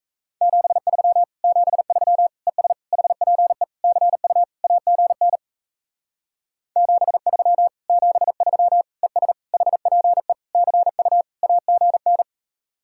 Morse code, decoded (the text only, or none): 73 73 ES HPE CU AGN 73 73 ES HPE CU AGN